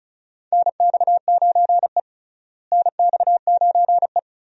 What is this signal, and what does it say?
Morse code, heard as a keyed tone, about 36 wpm, 700 Hz. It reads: NX9E NX9E
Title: USA Call Signs - 4 Characters - Repeated 3x - 35wpm